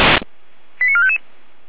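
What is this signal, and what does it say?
Five-tone selective call (ZVEI), heard as quick runs of five tones, tones of about 70 ms. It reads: 89455
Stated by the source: ZVEI five-tone selective call 89001 received by SDR, from the feuerwehr-orlamuende fire brigade audio server.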